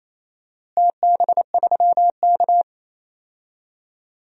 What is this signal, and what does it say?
Morse code, heard as a keyed tone, about 28 wpm, 700 Hz. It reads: TB3K
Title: DXCC-100 Call Signs - 4 Characters - Repeated 2x - 28wpm